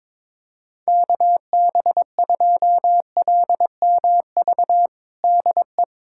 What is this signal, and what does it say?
Morse code, heard as a keyed tone, about 22 wpm, 700 Hz. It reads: KB2LMV DE
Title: USA Callsign DE Callsign - 6 Characters - 22wpm